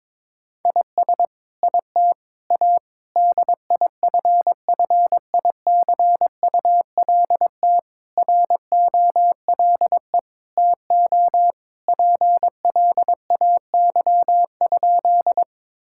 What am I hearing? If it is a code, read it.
Morse code, heard as a keyed tone, about 21 wpm, 700 Hz. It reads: IS IT A DIFFICULT ROLE TO PLAY?
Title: Sentences from Top 1000 Words - Review Part 6 - 22wpm